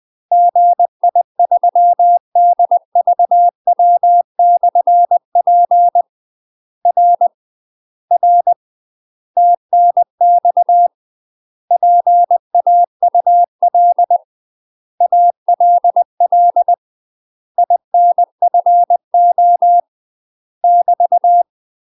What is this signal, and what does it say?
Morse code, heard as a keyed tone, about 20 wpm, 700 Hz. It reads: GI3DVW/P R R TNX PAUL ALL INFO <BT>
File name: UK MASTER QSOs TRAINING / 20WPM / UK MASTER QSOs 0029